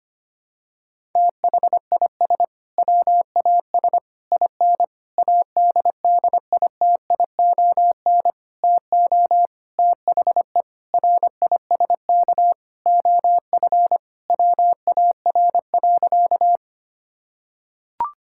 Morse code, heard as a keyed tone, about 25 wpm, 700 Hz. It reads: THIS WAS IN ADDITION TO THE RISK OF WAR.
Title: Sentences from Top 1000 Words - Review Part 6 - 25wpm